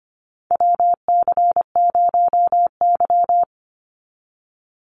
Morse code, heard as a keyed tone, about 25 wpm, 700 Hz. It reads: WC0Y
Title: NA SOTA Chaser Callsigns 25wpm